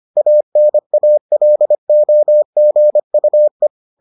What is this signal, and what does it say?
Morse code, heard as a keyed tone, about 25 wpm, 600 Hz. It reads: ANALOGUE